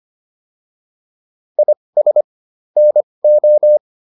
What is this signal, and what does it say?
Morse code, heard as a keyed tone, about 25 wpm, spaced as for 20 wpm, 600 Hz. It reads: IS NO